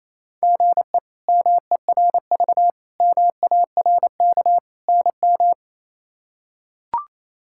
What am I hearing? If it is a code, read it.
Morse code, heard as a keyed tone, about 28 wpm, 700 Hz. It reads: GE MERV MARK NM
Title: K1USN SST QSO Practice - 28wpm